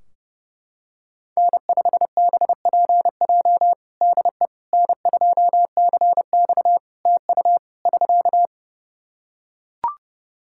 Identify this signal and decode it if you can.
Morse code, heard as a keyed tone, about 30 wpm, 700 Hz. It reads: N5BPJ DE N2CX TU <SK>
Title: K1USN SST QSO Practice 30wpm